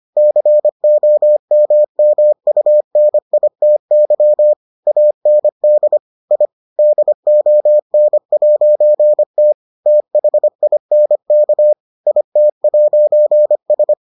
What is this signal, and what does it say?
Morse code, heard as a keyed tone, about 25 wpm, 600 Hz. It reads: COMMUNITY AND I DON'T THINK IT'S